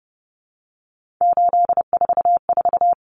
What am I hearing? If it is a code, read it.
Morse code, heard as a keyed tone, about 30 wpm, 700 Hz. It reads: 844
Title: Common Cut Numbers - 3-Digit 30wpm